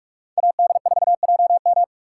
Morse code, heard as a keyed tone, about 58 wpm, 700 Hz. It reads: AD4JM